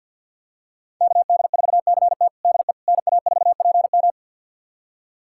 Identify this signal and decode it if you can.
Morse code, heard as a keyed tone, about 51 wpm, 700 Hz. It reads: KD4CT DE NA4PM
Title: USA Callsign DE Callsign - 5 Characters - 50wpm